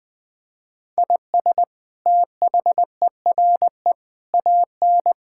Morse code, heard as a keyed tone, about 20 wpm, 700 Hz. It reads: IS THERE AN
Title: Sentences from Top 1600 Words 20wpm